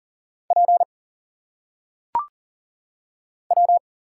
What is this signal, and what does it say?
Morse code, heard as a keyed tone, about 40 wpm, 700 Hz. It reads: P W
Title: Single Letters - Phonetic Pronunciation 40wpm